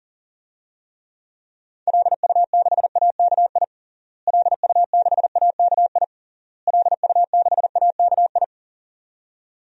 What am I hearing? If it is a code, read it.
Morse code, heard as a keyed tone, about 40 wpm, 700 Hz. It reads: LU6AKI LU6AKI LU6AKI
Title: DXCC-100 Call Signs - 6 Characters - Repeated 3x - 40wpm